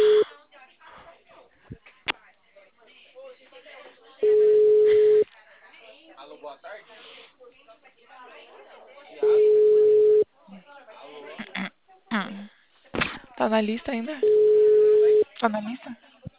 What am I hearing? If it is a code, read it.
Ringback tone (Europe, 425 Hz), with speech.
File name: out- -2046- - - .72728